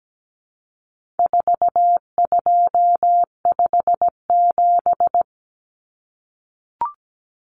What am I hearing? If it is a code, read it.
Morse code, heard as a keyed tone, about 17 wpm, 700 Hz. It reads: 4257
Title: Numbers - 4-Digit 17wpm